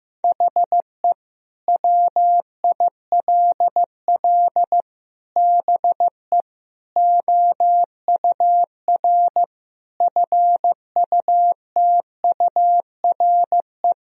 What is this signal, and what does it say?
Morse code, heard as a keyed tone, about 15 wpm, 700 Hz. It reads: HE WILL BE OUR FUTURE
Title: Sentences from Top 1000 Words - Review Part 3 - 15wpm